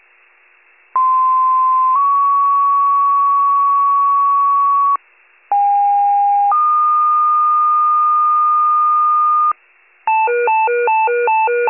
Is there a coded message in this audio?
Yes: two-tone page from 0.9 to 9.5 s, heard as a one-second tone, then a three-second tone, about 1040 then 1120 Hz, about 810 then 1220 Hz.